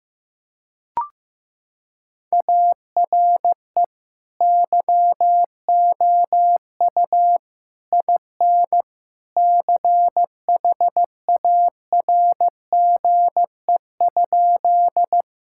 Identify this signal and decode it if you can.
Morse code, heard as a keyed tone, about 16 wpm, 700 Hz. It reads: ARE YOU IN CHARGE?
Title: Sentences from Top 1000 Words - Review Part 6 - 15wpm